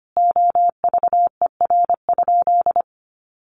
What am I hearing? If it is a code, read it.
Morse code, heard as a keyed tone, about 25 wpm, 700 Hz. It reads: OVER?
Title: Sentences from Top 1500 Words 25wpm